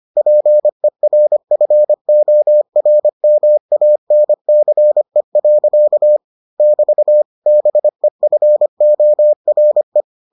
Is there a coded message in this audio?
Yes: Morse code from 0.2 to 10.0 s, heard as a keyed tone, about 25 wpm, 600 Hz. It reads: PERFORMANCE. <BT> BEFORE